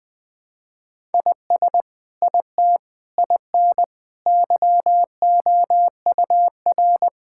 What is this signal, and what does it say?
Morse code, heard as a keyed tone, about 20 wpm, 700 Hz. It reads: IS IT IN YOUR
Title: Sentences from Top 1000 Words - Review Part 5 - 20wpm